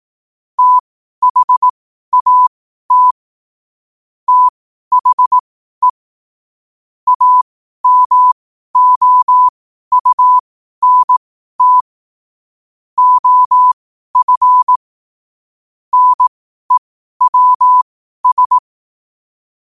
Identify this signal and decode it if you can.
Morse code, heard as a keyed tone, about 18 wpm, spaced as for 12 wpm, 1000 Hz. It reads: THAT THE AMOUNT OF NEWS
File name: qotdfiles / 12wpm